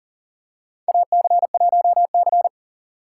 Morse code, heard as a keyed tone, about 40 wpm, 700 Hz. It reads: AC1C
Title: USA Call Signs - 3x Character Spacing - 40wpm